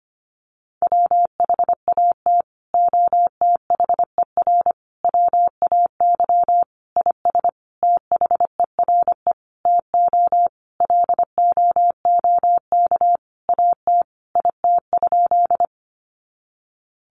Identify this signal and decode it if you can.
Morse code, heard as a keyed tone, about 25 wpm, 700 Hz. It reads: WHAT OTHER WAY IS THERE TO LOOK AT IT?
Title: Sentences from Top 100 Words 25wpm